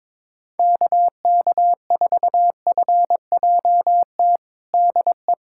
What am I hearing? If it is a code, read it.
Morse code, heard as a keyed tone, about 22 wpm, 700 Hz. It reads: KK4FJT DE
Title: USA Callsign DE Callsign - 6 Characters - 22wpm